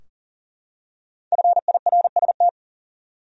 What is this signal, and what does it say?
Morse code, heard as a keyed tone, about 40 wpm, 700 Hz. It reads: FIRST
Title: Top 100 Words 40wpm